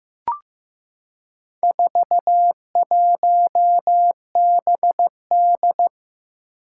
Morse code, heard as a keyed tone, about 15 wpm, 700 Hz. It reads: 41BD
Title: Four Challenging Letter-Number Combinations 15wpm